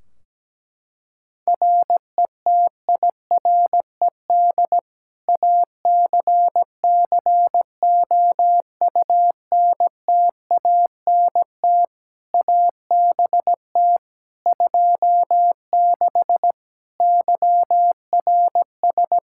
Morse code, heard as a keyed tone, about 17 wpm, 700 Hz. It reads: RETIRED ACCOUNTANT ABT 26 YRS
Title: Practice QSO - Exchange 3 - More Info - 17wpm